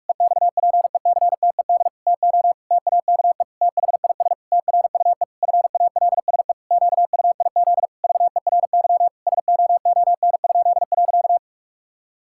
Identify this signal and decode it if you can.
Morse code, heard as a keyed tone, about 41 wpm, 700 Hz. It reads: EXPECTED TO TAKE THIS TRUE FALSE QUIZ VERY SOON?.
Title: Pangram Challenge v2 - 45wpm